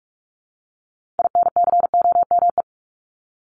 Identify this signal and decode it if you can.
Morse code, heard as a keyed tone, about 44 wpm, 700 Hz. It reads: INCOME
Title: Top 751-1000 Words 45wpm